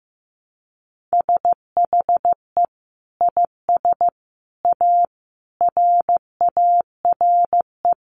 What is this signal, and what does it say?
Morse code, heard as a keyed tone, about 15 wpm, 700 Hz. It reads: SHE IS A RARE
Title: Sentences from Top 1100 Words 15wpm